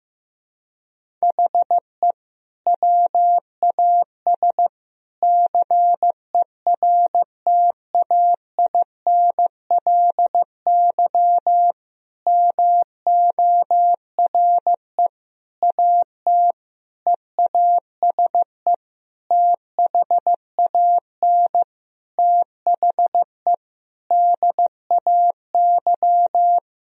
Morse code, heard as a keyed tone, about 15 wpm, 700 Hz. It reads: HE WAS CERTAINLY MORE AT EASE THAN THE DAY
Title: Sentences from Top 1000 Words - Review Part 1 - 15wpm